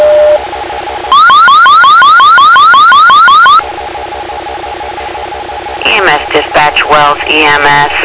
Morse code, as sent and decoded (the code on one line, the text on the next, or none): none